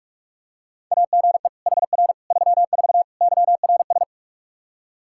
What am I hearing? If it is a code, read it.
Morse code, heard as a keyed tone, about 46 wpm, 700 Hz. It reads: AGE HR 34 YRS
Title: K1USN SST QSO Practice 45wpm